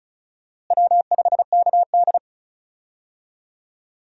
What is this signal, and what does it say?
Morse code, heard as a keyed tone, about 35 wpm, 700 Hz. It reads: W5KD
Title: USA Call Signs - 4 Characters - Repeated 3x - 35wpm